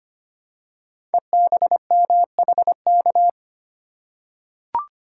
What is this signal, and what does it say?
Morse code, heard as a keyed tone, about 25 wpm, 700 Hz. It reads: EBMHK
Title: Code Groups - 5 Letters - 25wpm